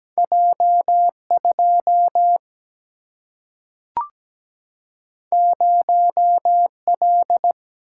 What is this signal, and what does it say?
Morse code, heard as a keyed tone, about 17 wpm, 700 Hz. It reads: J 2 0 L
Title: Single-Letter Single-Number 17wpm